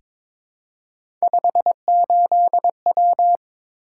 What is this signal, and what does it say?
Morse code, heard as a keyed tone, about 22 wpm, 700 Hz. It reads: 58W
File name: practice / 22wpm / Three Letter-Number Combinations - 22wpm